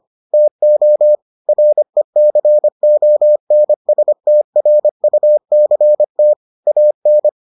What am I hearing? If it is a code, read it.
Morse code, heard as a keyed tone, about 25 wpm, 600 Hz. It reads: TO RECONSTRUCT AN